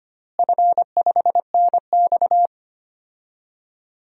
Morse code, heard as a keyed tone, about 25 wpm, 700 Hz. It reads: F5NX
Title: DXCC-100 Call Signs - 4 Characters - Repeated 3x - 25wpm